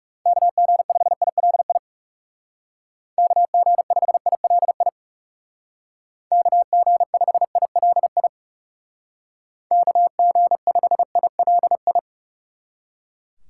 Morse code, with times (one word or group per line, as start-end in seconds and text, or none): 0.3-1.8 KG5ILI
3.2-4.9 KG5ILI
6.3-8.3 KG5ILI
9.7-12.0 KG5ILI